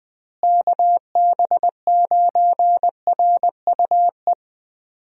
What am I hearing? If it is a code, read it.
Morse code, heard as a keyed tone, about 20 wpm, 700 Hz. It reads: KB9RUE
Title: USA Call Signs - 6 Characters - Repeated 3x - 20wpm